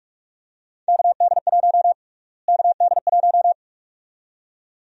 Morse code, heard as a keyed tone, about 47 wpm, 700 Hz. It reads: KD1 KD1
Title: USA Call Sign Prefixes - Repeated 2x - 45wpm